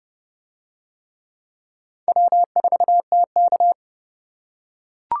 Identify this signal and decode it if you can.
Morse code, heard as a keyed tone, about 29 wpm, 700 Hz. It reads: W4TK